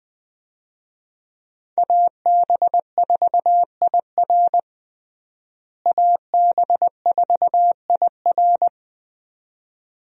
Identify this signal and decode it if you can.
Morse code, heard as a keyed tone, about 21 wpm, 700 Hz. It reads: AB4IR AB4IR
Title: USA Call Signs - 5 Characters - Repeated 2x - 20wpm